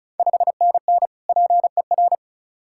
Morse code, heard as a keyed tone, about 36 wpm, 700 Hz. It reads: 5NN PER
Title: Ontario QSO Party Exchange 35wpm